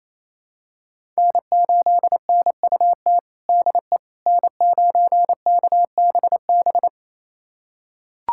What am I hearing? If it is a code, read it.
Morse code, heard as a keyed tone, about 27 wpm, 700 Hz. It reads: N8NUT DE N9KBB
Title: USA Callsign DE Callsign - 5 Characters - 28wpm